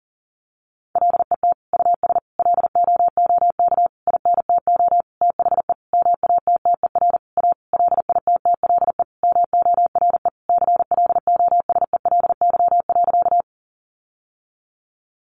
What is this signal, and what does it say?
Morse code, heard as a keyed tone, about 40 wpm, 700 Hz. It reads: LET US LOOK INTO THE MATTER A LITTLE MORE CLOSELY.
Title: Sentences from Top 1400 Words 40wpm